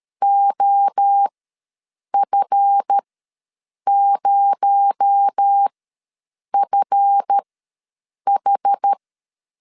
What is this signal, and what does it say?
Morse code, heard as a keyed tone, about 13 wpm, spaced as for 7 wpm, 800 Hz. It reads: OF0FH
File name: LEZIONI / 7 GF9Z / 7 GF ws 12wpm 3140